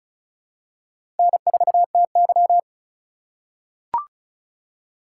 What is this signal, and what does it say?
Morse code, heard as a keyed tone, about 36 wpm, 700 Hz. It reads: N4TY
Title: USA Call Signs - 4x Character Spacing - 35wpm